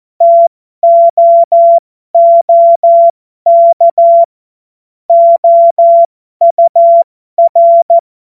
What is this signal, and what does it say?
Morse code, heard as a keyed tone, about 14 wpm, spaced as for 12 wpm, 680 Hz. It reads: TOOK OUR